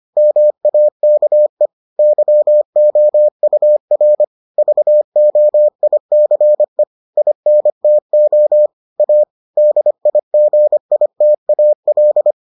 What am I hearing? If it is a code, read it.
Morse code, heard as a keyed tone, about 25 wpm, 600 Hz. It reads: MAKE YOUR VOICE INTO A DIGITAL